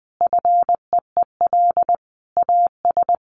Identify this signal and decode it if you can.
Morse code, heard as a keyed tone, about 19 wpm, 700 Hz. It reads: FEEL AS
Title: Sentences from Top 1500 Words 20wpm